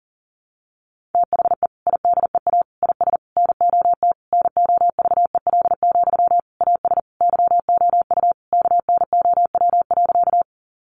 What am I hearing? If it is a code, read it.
Morse code, heard as a keyed tone, about 40 wpm, 700 Hz. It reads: THE IDEA IS NOT NOVEL, AS YOU KNOW.
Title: Sentences from Top 1300 Words 40wpm